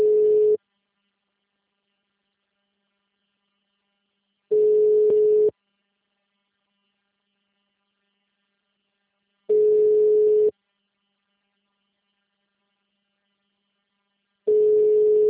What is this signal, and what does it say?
Ringback tone (Europe, 425 Hz).
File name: out- -2022- - - .510